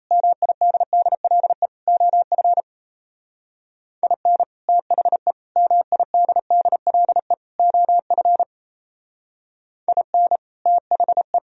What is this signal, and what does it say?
Morse code, heard as a keyed tone, about 35 wpm, 700 Hz. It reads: MIDDLE OF IN THE MIDDLE OF IN THE